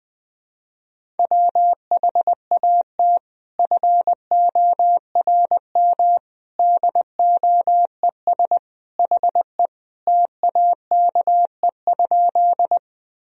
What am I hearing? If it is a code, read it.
Morse code, heard as a keyed tone, about 20 wpm, 700 Hz. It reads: WHAT FORM DOES HE TAKE?
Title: Sentences from Top 1000 Words - Review Part 1 - 20wpm